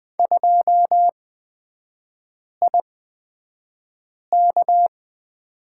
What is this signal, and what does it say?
Morse code, heard as a keyed tone, about 20 wpm, spaced as for 5 wpm, 700 Hz. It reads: 2IK
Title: 243 Lesson 39e - Letter z - Three Characters 20wpm